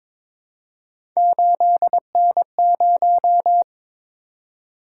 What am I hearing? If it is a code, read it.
Morse code, heard as a keyed tone, about 22 wpm, 700 Hz. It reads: 8N0
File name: practice / 22wpm / Three Letter-Number Combinations - 22wpm